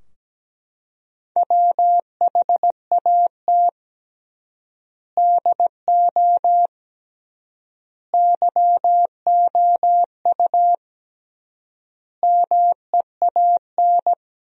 Morse code, heard as a keyed tone, about 17 wpm, 700 Hz. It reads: WHAT DO YOU MEAN
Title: 3x Word-Spacing - Sentences from Top 200 Words - 17wpm